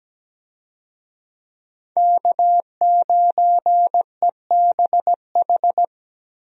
Morse code, heard as a keyed tone, about 17 wpm, 700 Hz. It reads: K9EBH
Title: USA Call Signs - 5 Characters - Repeated 2x - 17wpm